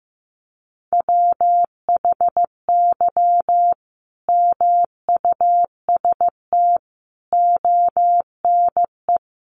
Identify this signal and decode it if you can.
Morse code, heard as a keyed tone, about 15 wpm, 700 Hz. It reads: WHY MUST ONE
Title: Sentences from Top 1400 Words 15wpm